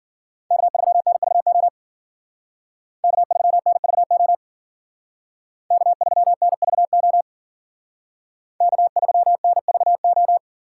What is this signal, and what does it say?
Morse code, heard as a keyed tone, about 48 wpm, 700 Hz. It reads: K3NVO K3NVO K3NVO K3NVO
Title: Speed Racing - USA 1x3 Call Signs 40wpm